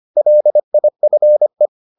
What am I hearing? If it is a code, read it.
Morse code, heard as a keyed tone, about 25 wpm, 600 Hz. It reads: LIFE